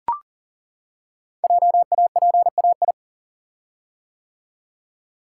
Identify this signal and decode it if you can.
Morse code, heard as a keyed tone, about 42 wpm, 690 Hz. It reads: JAPAI
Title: Code Groups - 5 Letter:Numbers - 40wpm